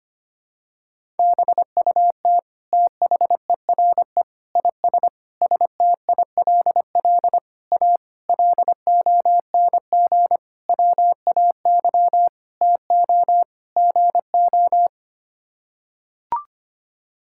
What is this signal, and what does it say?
Morse code, heard as a keyed tone, about 25 wpm, 700 Hz. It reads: BUT THERE IS STILL A LONG WAY TO GO
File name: Sets of 9 Words - 25wpm